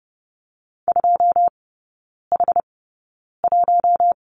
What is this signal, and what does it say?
Morse code, heard as a keyed tone, about 30 wpm, spaced as for 9 wpm, 700 Hz. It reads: 2H1
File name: Lesson 17e - Letter u - Three Characters 30wpm